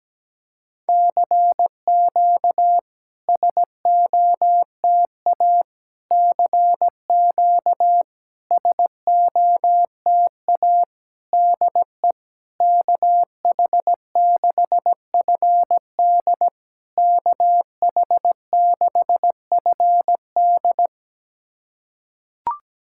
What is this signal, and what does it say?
Morse code, heard as a keyed tone, about 16 wpm, 700 Hz. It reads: CQ SOTA CQ SOTA DE KH6FD KH6FD
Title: SOTA QSO Practice 17wpm